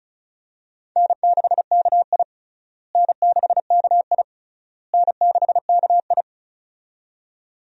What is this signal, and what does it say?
Morse code, heard as a keyed tone, about 34 wpm, 700 Hz. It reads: N6KI N6KI N6KI
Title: USA Call Signs - 4 Characters - Repeated 3x - 35wpm